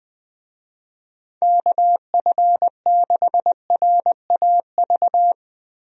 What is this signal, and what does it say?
Morse code, heard as a keyed tone, about 20 wpm, 700 Hz. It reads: KF6RAV
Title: USA Call Signs - 6 Characters - 20wpm